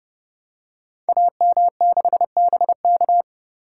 Morse code, heard as a keyed tone, about 30 wpm, 700 Hz. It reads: AM6BK
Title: Code Groups - 5 Letter:Numbers - 30wpm